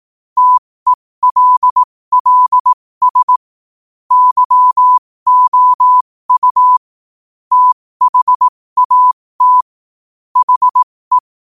Morse code, heard as a keyed tone, about 18 wpm, spaced as for 15 wpm, 1000 Hz. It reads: TELLS YOU THAT HE
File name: qotdfiles / 15wpm